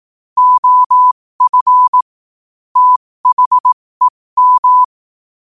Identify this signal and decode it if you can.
Morse code, heard as a keyed tone, about 18 wpm, spaced as for 15 wpm, 1000 Hz. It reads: OF THEM
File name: qotdfiles / 15wpm